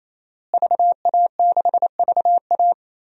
Morse code, heard as a keyed tone, about 28 wpm, 700 Hz. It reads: VA6VA
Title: DXCC-100 Call Signs - 5 Characters - Repeated 3x - 28wpm